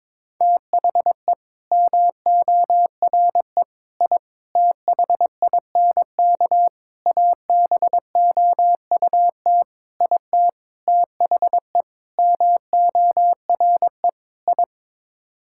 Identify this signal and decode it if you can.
Morse code, heard as a keyed tone, about 22 wpm, 700 Hz. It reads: THE MORE I THINK ABOUT IT THE MORE I
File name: Sets of 9 Words - 22wpm